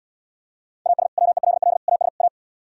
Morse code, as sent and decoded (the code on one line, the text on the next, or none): .- ... / .-- . .-.. .-.. / .- ... / .-
AS WELL AS A